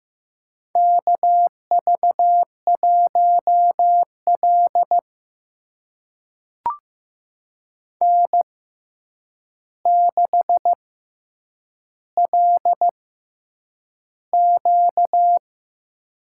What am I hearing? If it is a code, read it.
Morse code, heard as a keyed tone, about 15 wpm, spaced as for 8 wpm, 700 Hz. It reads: KV1L N6LQ
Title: USA Call Signs - 4 Characters - 6x Character Spacing - 15wpm